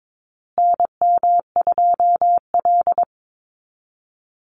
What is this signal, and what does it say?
Morse code, heard as a keyed tone, about 22 wpm, 700 Hz. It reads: NM2L
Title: Call Signs Repeated 3x - 22wpm